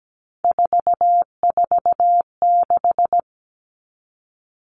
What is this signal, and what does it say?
Morse code, heard as a keyed tone, about 17 wpm, 700 Hz. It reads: 446
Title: Numbers - 3-Digit 17wpm